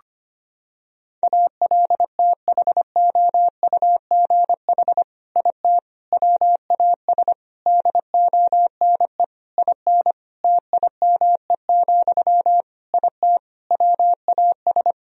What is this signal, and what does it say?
Morse code, heard as a keyed tone, about 25 wpm, 700 Hz. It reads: ALTHOUGH IT WAS DONE IN TIME, IT WAS
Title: Sentences from Top 1000 Words - Review Part 3 - 25wpm